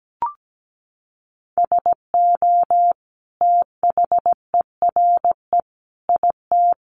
Morse code, heard as a keyed tone, about 17 wpm, 700 Hz. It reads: SO THERE IT
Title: Sentences from Top 100 Words 17wpm